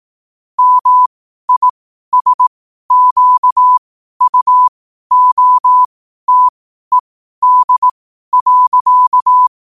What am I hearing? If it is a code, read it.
Morse code, heard as a keyed tone, about 18 wpm, spaced as for 12 wpm, 1000 Hz. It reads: MISQUOTED.
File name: qotdfiles / 12wpm